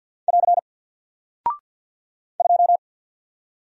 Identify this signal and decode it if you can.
Morse code, heard as a keyed tone, about 48 wpm, 700 Hz. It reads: <AR> 2
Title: Single Character (Letter:Number:Punctuation:Prosign) 50wpm